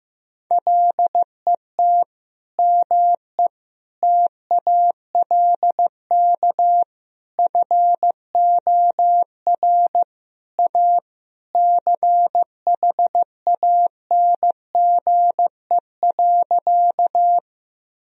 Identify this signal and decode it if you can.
Morse code, heard as a keyed tone, about 15 wpm, 700 Hz. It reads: LET ME TALK FOR A CHANGE.
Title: Sentences from Top 1000 Words - Review Part 1 - 15wpm